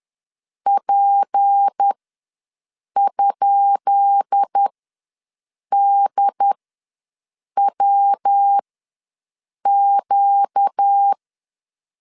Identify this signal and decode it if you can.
Morse code, heard as a keyed tone, about 10 wpm, spaced as for 6 wpm, 800 Hz. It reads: P?DWQ
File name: LEZIONI / 10 SIGNS / 10B INT ws 10wpm 4134